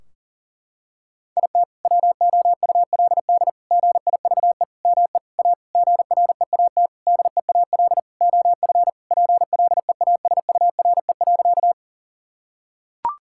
Morse code, heard as a keyed tone, about 40 wpm, 700 Hz. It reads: IT WOULD GIVE ME A GREAT DEAL OF PLEASURE.
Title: Sentences from Top 1600 Words 40wpm